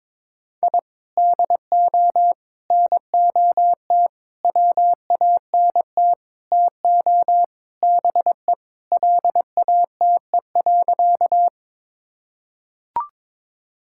Morse code, heard as a keyed tone, about 22 wpm, 700 Hz. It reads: I DO NOT WANT TO BE LATE.
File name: practice / 22wpm / Sentences with 4-Word Phrases - 22wpm